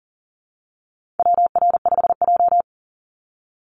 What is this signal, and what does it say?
Morse code, heard as a keyed tone, about 40 wpm, 700 Hz. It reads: WR5J
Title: Call Signs Repeated 2x - 40wpm